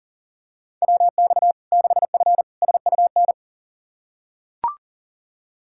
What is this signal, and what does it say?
Morse code, heard as a keyed tone, about 40 wpm, 700 Hz. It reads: WX 6F SUN
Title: Practice QSO - Exchange 2 - Info - 40wpm